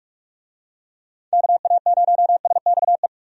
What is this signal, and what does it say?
Morse code, heard as a keyed tone, about 46 wpm, 700 Hz. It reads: KA0SXE
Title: USA Call Signs - 6 Characters - 45wpm